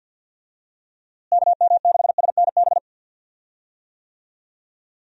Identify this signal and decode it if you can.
Morse code, heard as a keyed tone, about 50 wpm, 700 Hz. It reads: KM6SNB
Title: USA Call Signs - 6 Characters - Repeated 2x - 50wpm